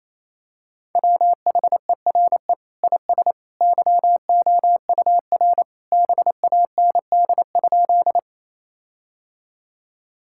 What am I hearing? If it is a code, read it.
Morse code, heard as a keyed tone, about 28 wpm, 700 Hz. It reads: WHERE IS YOUR BAND?
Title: Sentences from Top 1600 Words 28wpm